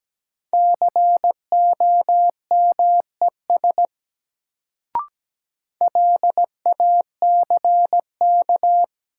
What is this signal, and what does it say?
Morse code, heard as a keyed tone, about 17 wpm, 700 Hz. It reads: COMES LACK
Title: Sets of 3 Words - New Word Drill - 17wpm